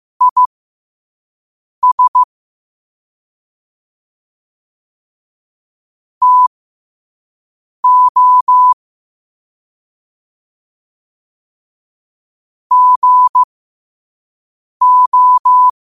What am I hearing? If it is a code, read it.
Morse code, heard as a keyed tone, about 15 wpm, spaced as for 5 wpm, 1000 Hz. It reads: IS TO GO
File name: qotdfiles / 5wpm